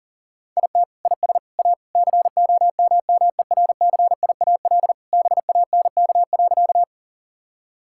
Morse code, heard as a keyed tone, about 40 wpm, 700 Hz. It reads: IT IS A COMMERCIAL BANK.
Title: Sentences from Top 1000 Words - Review Part 6 - 40wpm